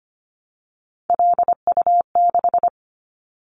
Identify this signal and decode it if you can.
Morse code, heard as a keyed tone, about 25 wpm, 690 Hz. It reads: LU6
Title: Callsign Prefixes for Top 50 DXCC - 25wpm